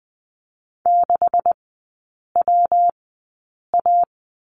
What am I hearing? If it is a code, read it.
Morse code, heard as a keyed tone, about 20 wpm, spaced as for 8 wpm, 700 Hz. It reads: 6WA
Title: 127 Lesson 21e - Number 6 - Three Characters 20wpm